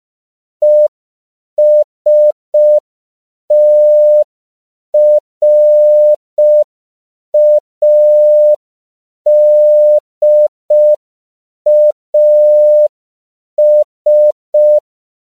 Morse code, heard as a keyed tone, about 5 wpm, 600 Hz. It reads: ESTRADAS
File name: 5PPM